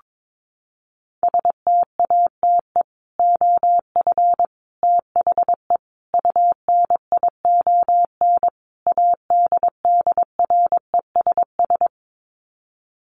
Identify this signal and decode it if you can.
Morse code, heard as a keyed tone, about 22 wpm, 700 Hz. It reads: STATE OF THE UNION ADDRESS
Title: Sets of 5 Words Encore 22wpm